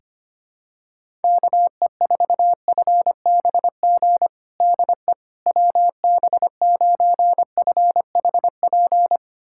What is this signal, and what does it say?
Morse code, heard as a keyed tone, about 25 wpm, 700 Hz. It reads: KE4FBG DE WB9FHP
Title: USA Callsign DE Callsign - 6 Characters - 25wpm